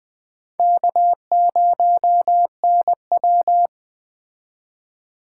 Morse code, heard as a keyed tone, about 20 wpm, 700 Hz. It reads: K0NW
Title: Speed Racing - USA 1x2 Call Signs 17wpm